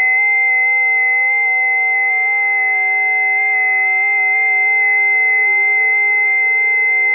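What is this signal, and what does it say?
FT8.
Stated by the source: FT8 signals recorded on a busy 20 m band.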